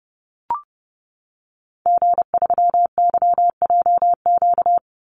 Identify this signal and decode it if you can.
Morse code, heard as a keyed tone, about 30 wpm, 700 Hz. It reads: G3YJQ
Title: Call Signs - 30wpm